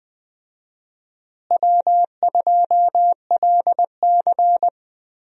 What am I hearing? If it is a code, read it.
Morse code, heard as a keyed tone, about 20 wpm, 700 Hz. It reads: W2LC
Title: USA Call Signs - 4 Characters - Repeated 2x - 20wpm